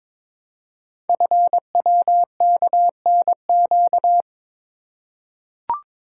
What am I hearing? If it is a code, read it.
Morse code, heard as a keyed tone, about 22 wpm, 700 Hz. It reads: FWKNQ
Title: Code Groups - 5 Letters - 22wpm